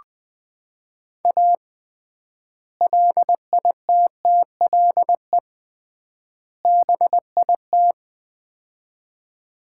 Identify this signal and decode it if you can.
Morse code, heard as a keyed tone, about 20 wpm, 700 Hz. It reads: A LITTLE BIT
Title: 3x Word-Spacing - Sets of 3 Words - 20wpm